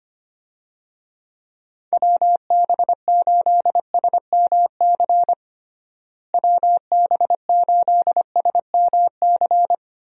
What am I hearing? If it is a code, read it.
Morse code, heard as a keyed tone, about 25 wpm, 700 Hz. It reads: WB8SMC WB8SMC
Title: USA Call Signs - 6 Characters - Repeated 2x - 25wpm